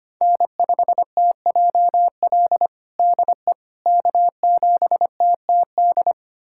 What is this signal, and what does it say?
Morse code, heard as a keyed tone, about 25 wpm, 700 Hz. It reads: N5TJL DE K7TTD
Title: USA Callsign DE Callsign - 5 Characters - 25wpm